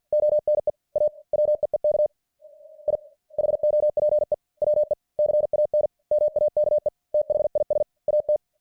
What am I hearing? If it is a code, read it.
Morse code, heard as a keyed tone, about 50 wpm, 600 Hz. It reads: ONE A WEEK I HOPE WE CAN MAKE THIS AT